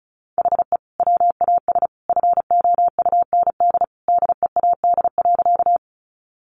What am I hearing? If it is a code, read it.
Morse code, heard as a keyed tone, about 34 wpm, 700 Hz. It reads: HE WAS FOUND DEAD.
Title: Sentences from Top 1500 Words 35wpm